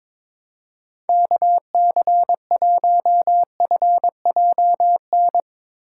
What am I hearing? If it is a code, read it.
Morse code, heard as a keyed tone, about 23 wpm, 700 Hz. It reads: KC1FJN
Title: Speed Racing - USA 2x3 Call Signs 22wpm